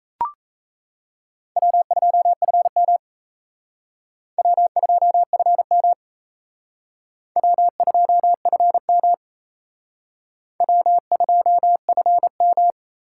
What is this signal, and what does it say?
Morse code, heard as a keyed tone, about 34 wpm, 700 Hz. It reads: W2FM W2FM W2FM W2FM
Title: Speed Racing - USA Call Signs 28wpm